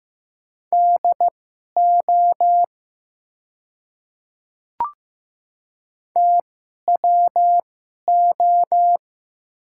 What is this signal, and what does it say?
Morse code, heard as a keyed tone, about 15 wpm, spaced as for 11 wpm, 700 Hz. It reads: DO TWO
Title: 2x Character Spacing - Top 200 Words - 15wpm